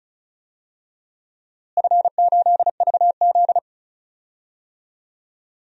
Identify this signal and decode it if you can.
Morse code, heard as a keyed tone, about 36 wpm, 700 Hz. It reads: F8VZ
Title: DXCC-100 Call Signs - 4 Characters - Repeated 3x - 35wpm